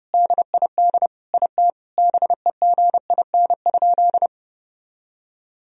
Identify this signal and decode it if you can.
Morse code, heard as a keyed tone, about 30 wpm, 700 Hz. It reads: DID IT BEGIN?
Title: Sentences from Top 300 Words and Abbreviations 30wpm